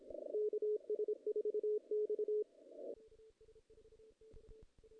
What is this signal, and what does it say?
Morse code, heard as a keyed tone, about 28 wpm, 420 Hz. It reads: KS4X
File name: 2016 JAN CW NAQP / 2016 Jan naqp k5ot - 0-7